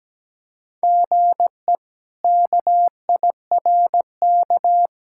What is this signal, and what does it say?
Morse code, heard as a keyed tone, about 17 wpm, 700 Hz. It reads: GE KIRK
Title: K1USN SST QSO Practice - 17wpm